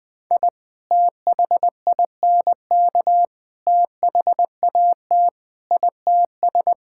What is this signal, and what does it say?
Morse code, heard as a keyed tone, about 20 wpm, 700 Hz. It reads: I THINK THAT ITS
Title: Sentences with 2-Word Phrases - 20wpm